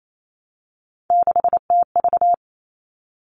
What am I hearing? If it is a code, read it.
Morse code, heard as a keyed tone, about 28 wpm, 700 Hz. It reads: 6TV
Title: All Cut Numbers - 3-Digit 28wpm